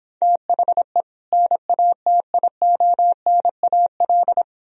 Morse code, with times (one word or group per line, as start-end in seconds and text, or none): 0.2-1.0 THE
1.3-4.4 NATIONAL